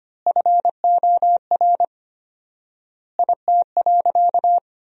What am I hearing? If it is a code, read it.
Morse code, heard as a keyed tone, about 25 wpm, 700 Hz. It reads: FOR IT.
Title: 4x Word-Spacing - Sentences from Top 300 Words - 25wpm